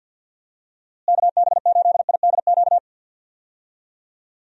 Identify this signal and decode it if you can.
Morse code, heard as a keyed tone, about 51 wpm, 700 Hz. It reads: KB8IDQ
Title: USA Call Signs - 6 Characters - Repeated 3x - 50wpm